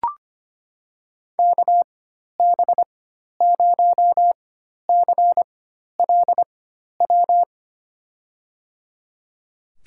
Morse code, heard as a keyed tone, about 25 wpm, spaced as for 12 wpm, 700 Hz. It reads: KB0CLW
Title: USA Call Signs - 6 Characters - 4x Character Spacing - 25wpm